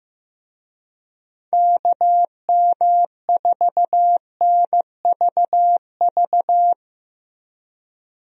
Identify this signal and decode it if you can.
Morse code, heard as a keyed tone, about 15 wpm, 700 Hz. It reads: KM4NVV
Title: USA Call Signs - 6 Characters - Repeated 2x - 15wpm